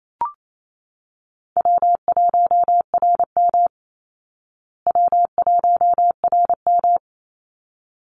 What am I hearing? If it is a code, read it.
Morse code, heard as a keyed tone, about 28 wpm, 700 Hz. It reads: W1RM W1RM
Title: Easy Call Signs Repeated 2x - 28wpm